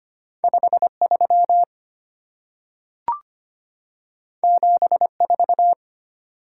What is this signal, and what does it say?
Morse code, heard as a keyed tone, about 25 wpm, 700 Hz. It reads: 53 74
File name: Two Letter-Number Combinations - 25wpm